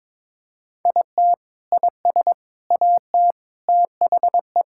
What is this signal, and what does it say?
Morse code, heard as a keyed tone, about 22 wpm, 700 Hz. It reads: IT IS AT THE
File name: practice / 22wpm / Sentences with 4-Word Phrases - 22wpm